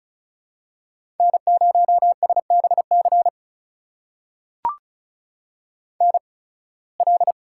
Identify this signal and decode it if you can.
Morse code, heard as a keyed tone, about 36 wpm, spaced as for 23 wpm, 700 Hz. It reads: N0SBC NL
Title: USA Call Signs - 5 Characters - 8x Character Spacing - 35wpm